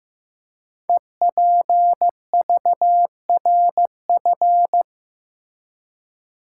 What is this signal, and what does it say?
Morse code, heard as a keyed tone, about 15 wpm, 700 Hz. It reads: EPVRF
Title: Code Groups - 5 Letters - 15wpm